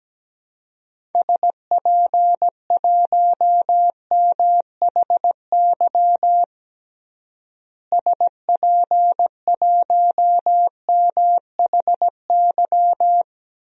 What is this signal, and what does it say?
Morse code, heard as a keyed tone, about 17 wpm, 700 Hz. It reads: SP1MHY SP1MHY
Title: DXCC-100 Call Signs - 6 Characters - Repeated 3x - 17wpm